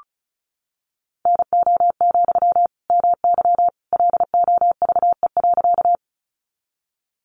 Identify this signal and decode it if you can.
Morse code, heard as a keyed tone, about 34 wpm, 700 Hz. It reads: NO, MY LOVE.
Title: Sentences from Top 400 Words 35wpm